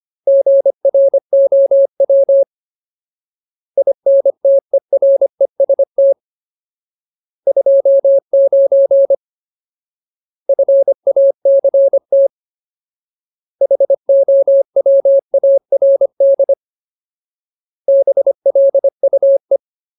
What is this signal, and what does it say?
Morse code, heard as a keyed tone, about 25 wpm, 550 Hz. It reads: GROW INTEREST 29 FACT HOWARD BLUE
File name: Ultimate Word List / 25WPM Ultimate